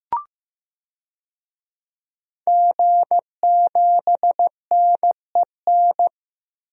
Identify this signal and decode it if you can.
Morse code, heard as a keyed tone, about 15 wpm, 700 Hz. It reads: G7NEN
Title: DXCC-100 Call Signs - 5 Characters - 15wpm